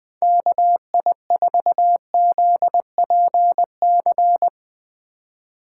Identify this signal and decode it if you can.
Morse code, heard as a keyed tone, about 20 wpm, 700 Hz. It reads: KI4ZPC
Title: USA Call Signs - 6 Characters - 2x Character Spacing - 20wpm